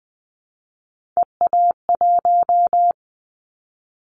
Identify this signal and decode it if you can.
Morse code, heard as a keyed tone, about 19 wpm, 700 Hz. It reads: EA1
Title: Callsign Prefixes for Top 25 DXCC - 20wpm